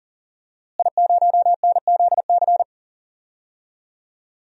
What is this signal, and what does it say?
Morse code, heard as a keyed tone, about 40 wpm, 700 Hz. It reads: I0NZC
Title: DXCC-100 Call Signs - 5 Characters - Repeated 3x - 40wpm